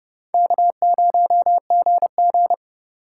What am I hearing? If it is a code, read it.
Morse code, heard as a keyed tone, about 32 wpm, 700 Hz. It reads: K0GG
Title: USA Call Signs - 4 Characters - 4x Character Spacing - 30wpm